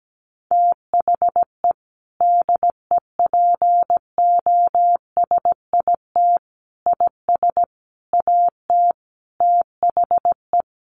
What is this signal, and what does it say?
Morse code, heard as a keyed tone, about 17 wpm, 700 Hz. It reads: THE DEPOSIT IS AT THE
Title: Sentences from Top 1300 Words 17wpm